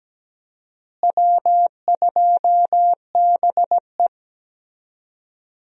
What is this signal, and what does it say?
Morse code, heard as a keyed tone, about 17 wpm, 700 Hz. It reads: W2BE
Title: USA Call Signs - 4 Characters - Repeated 2x - 17wpm